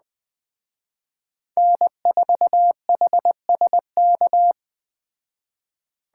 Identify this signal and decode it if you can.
Morse code, heard as a keyed tone, about 19 wpm, 700 Hz. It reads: N4HSK